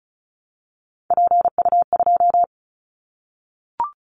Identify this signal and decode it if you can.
Morse code, heard as a keyed tone, about 34 wpm, 700 Hz. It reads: PU2
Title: Callsign Prefixes for Top 100 DXCC - 35wpm